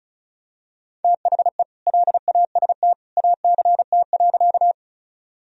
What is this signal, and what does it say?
Morse code, heard as a keyed tone, about 34 wpm, 700 Hz. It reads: THE LAST ACT.
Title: Sentences from Top 1000 Words - Review Part 1 - 35wpm